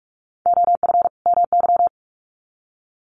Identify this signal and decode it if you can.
Morse code, heard as a keyed tone, about 44 wpm, 700 Hz. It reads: OF MY
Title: Sets of 2 Words 45wpm